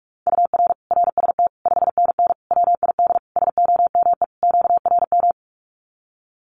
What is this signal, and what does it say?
Morse code, heard as a keyed tone, about 44 wpm, 700 Hz. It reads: UR RST 5NN WID SOME QRM
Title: Common QSO Phrases 45wpm